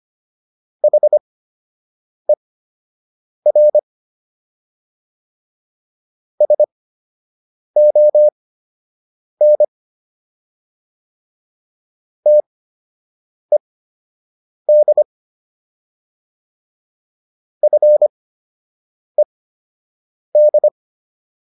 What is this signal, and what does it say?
Morse code, heard as a keyed tone, about 25 wpm, spaced as for 7 wpm, 600 Hz. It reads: HER SON TED FED